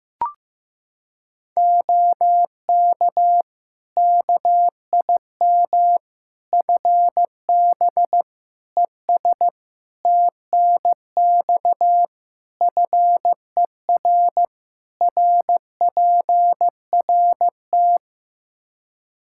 Morse code, heard as a keyed tone, about 15 wpm, 700 Hz. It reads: OK KIM FB ES TNX FER RPRT
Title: Practice QSO - Exchange 2 - Info - 15wpm